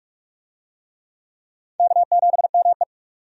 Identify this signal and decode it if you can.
Morse code, heard as a keyed tone, about 46 wpm, 700 Hz. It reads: K7ME
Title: USA Call Signs - 4 Characters - 45wpm